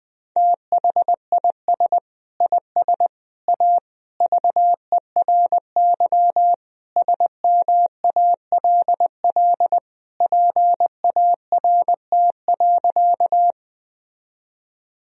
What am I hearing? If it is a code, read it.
Morse code, heard as a keyed tone, about 20 wpm, 700 Hz. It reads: THIS IS A VERY SMALL PART.
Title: Sentences from Top 1000 Words - Review Part 1 - 20wpm